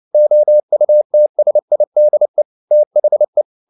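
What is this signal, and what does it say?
Morse code, heard as a keyed tone, about 29 wpm, 600 Hz. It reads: OUTSIDE THE
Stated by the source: Morse code at 29 wpm, sending Ken Follett's novel Never in English, from CW med Gnister.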